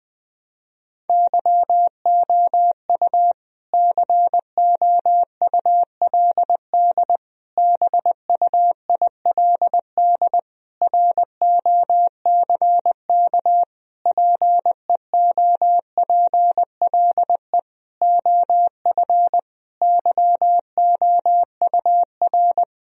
Morse code, heard as a keyed tone, about 20 wpm, 700 Hz. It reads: YOU COULD BUILD ROCK PEOPLE OF YOUR
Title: Sentences from Top 1000 Words - Review Part 2 - 20wpm